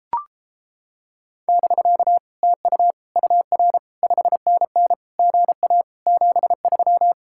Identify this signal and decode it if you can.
Morse code, heard as a keyed tone, about 29 wpm, 700 Hz. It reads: <BK> TU UR 5NN GA 73